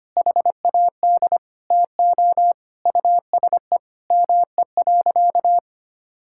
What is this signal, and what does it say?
Morse code, heard as a keyed tone, about 25 wpm, 700 Hz. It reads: HAD TO USE ME.
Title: Sentences from Top 1000 Words - Review Part 6 - 25wpm